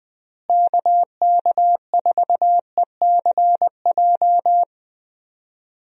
Morse code, heard as a keyed tone, about 20 wpm, 700 Hz. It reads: KK4ECJ
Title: USA Call Signs - 6 Characters - Repeated 3x - 20wpm